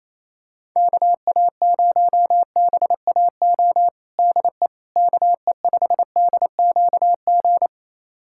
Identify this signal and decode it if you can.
Morse code, heard as a keyed tone, about 28 wpm, 700 Hz. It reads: KA0BAO DE KE5DQG
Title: USA Callsign DE Callsign 28wpm